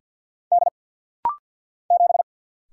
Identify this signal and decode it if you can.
Morse code, heard as a keyed tone, about 48 wpm, 700 Hz. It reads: D 7
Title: Single Letter-Number Alt - Rapid-Fire 50wpm